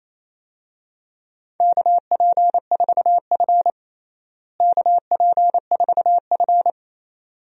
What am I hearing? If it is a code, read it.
Morse code, heard as a keyed tone, about 28 wpm, 700 Hz. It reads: KP4F KP4F
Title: DXCC-100 Call Signs - 4 Characters - Repeated 2x - 28wpm